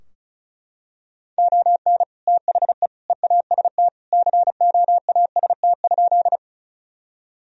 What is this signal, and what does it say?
Morse code, heard as a keyed tone, about 34 wpm, 700 Hz. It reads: ON THE EAST COAST?
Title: Sentences from Top 1000 Words - Review Part 4 - 35wpm